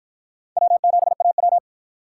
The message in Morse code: W7AW

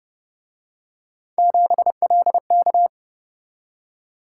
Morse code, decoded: 7LK

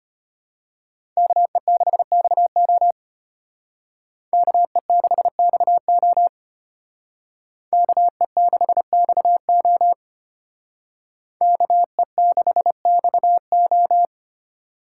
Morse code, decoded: KE6XO KE6XO KE6XO KE6XO

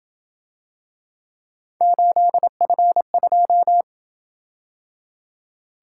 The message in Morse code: 8F2

